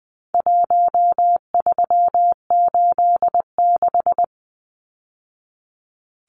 Morse code, heard as a keyed tone, about 20 wpm, 700 Hz. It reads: 1386
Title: Numbers - Contest Style - 4-Digit 20wpm